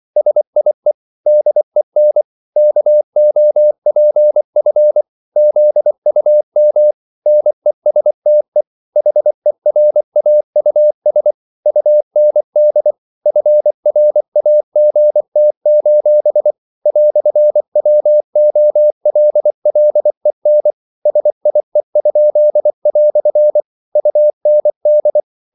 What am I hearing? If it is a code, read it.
Morse code, heard as a keyed tone, about 24 wpm, 600 Hz. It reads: SIE DEN KOPF ZUM NESTE HERAUS UND FRAGT: "WOLLEN SIE?" UND